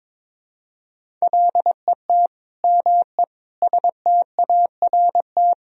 Morse code, heard as a keyed tone, about 22 wpm, 700 Hz. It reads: LET ME START
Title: Sentences from Top 1000 Words - Review Part 1 - 22wpm